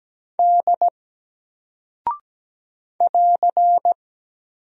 Morse code, heard as a keyed tone, about 17 wpm, 700 Hz. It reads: D <AR>
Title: Single Character (Letter:Number:Punctuation:Prosign) Alt 17wpm